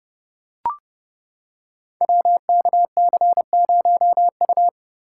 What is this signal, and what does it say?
Morse code, heard as a keyed tone, about 30 wpm, 700 Hz. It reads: WKC0U
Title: Code Groups - 5 Letter:Numbers - 30wpm